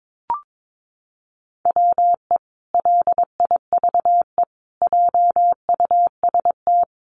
Morse code, heard as a keyed tone, about 22 wpm, 700 Hz. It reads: WE LIVE JUST
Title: Sentences from Top 1300 Words 22wpm